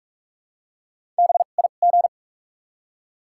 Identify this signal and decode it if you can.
Morse code, heard as a keyed tone, about 46 wpm, spaced as for 32 wpm, 700 Hz. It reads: BIG
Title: 2x Character Spacing - Top 200 Words - 45wpm